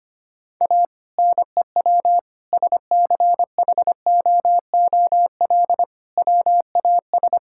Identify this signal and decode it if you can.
Morse code, heard as a keyed tone, about 25 wpm, 700 Hz. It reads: A NEW SCHOOL WAS
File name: practice / 25wpm / Sentences with 2-Word Phrases - 25wpm